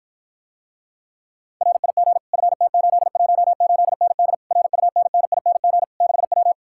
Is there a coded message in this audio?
Yes: Morse code from 1.6 to 6.5 s, heard as a keyed tone, about 54 wpm, 700 Hz. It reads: RIG FT818ND RUNNING 6W